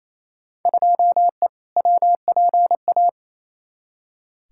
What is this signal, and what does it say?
Morse code, heard as a keyed tone, about 28 wpm, 700 Hz. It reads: 2E WPA
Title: Field Day Exchanges Encore - 28wpm